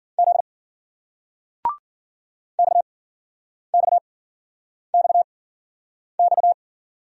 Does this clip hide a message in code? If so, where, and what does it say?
Yes: Morse code from 0.2 to 6.5 s, heard as a keyed tone, about 56 wpm, spaced as for 9 wpm, 700 Hz. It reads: Z XXXX